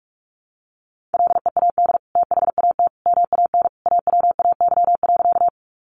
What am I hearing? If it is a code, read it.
Morse code, heard as a keyed tone, about 44 wpm, 700 Hz. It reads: LEAD THAT MAN AWAY.
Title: Sentences from Top 600 Words 45wpm